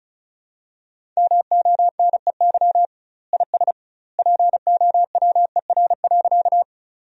Morse code, heard as a keyed tone, about 36 wpm, 700 Hz. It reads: MONEY IS POWER.
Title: 2x Word-Spacing - Sentences from Top 500 Words - 35wpm